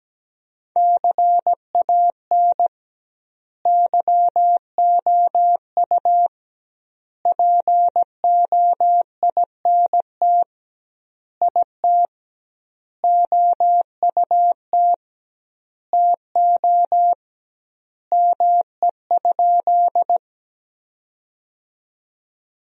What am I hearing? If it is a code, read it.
Morse code, heard as a keyed tone, about 17 wpm, 700 Hz. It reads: CAN YOU POINT IT OUT TO ME?
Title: 2x Word-Spacing - Sentences from Top 200 Words - 17wpm